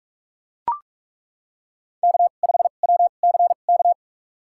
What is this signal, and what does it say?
Morse code, heard as a keyed tone, about 42 wpm, spaced as for 31 wpm, 700 Hz. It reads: K5WCK
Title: USA Call Signs - 2x Character Spacing - 45wpm